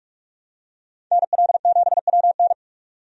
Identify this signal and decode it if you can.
Morse code, heard as a keyed tone, about 47 wpm, 700 Hz. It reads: NL7WN
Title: USA Call Signs - 5 Characters - 45wpm